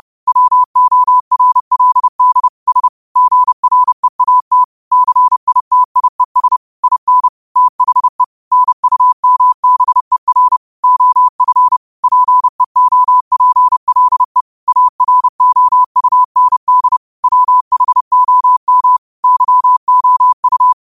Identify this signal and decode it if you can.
Morse code, heard as a keyed tone, about 30 wpm, 1000 Hz. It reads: WORLD S GREAT CITIES IN THE NUMBER OF PEOPLE AROUND WHOM YOU